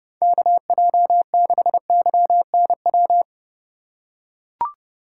Morse code, heard as a keyed tone, about 29 wpm, 700 Hz. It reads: KJ6YNW